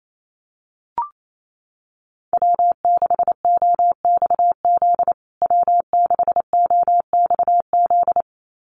Morse code, heard as a keyed tone, about 28 wpm, 700 Hz. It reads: W6OXZ W6OXZ
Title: USA Call Signs Repeated 2x 28wpm